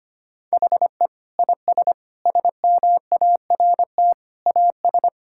Morse code, heard as a keyed tone, about 25 wpm, 700 Hz. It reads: HE IS SMART AS
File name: practice / 25wpm / Sentences with 4-Word Phrases - 25wpm